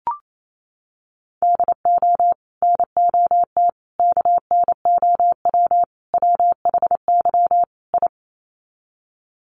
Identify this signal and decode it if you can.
Morse code, heard as a keyed tone, about 28 wpm, 700 Hz. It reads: DO NOT KNOW WHY I
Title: Sets of 5 Words Encore 28wpm